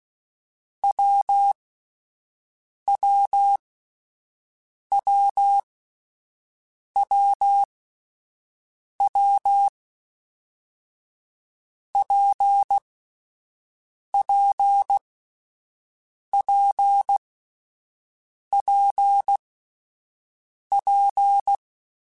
Morse code, heard as a keyed tone, about 16 wpm, spaced as for 5 wpm, 800 Hz. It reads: WWWWW PPPPP
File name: Difficult Combos / Difficult Combos-mp3s / WPJ WPJ1 @5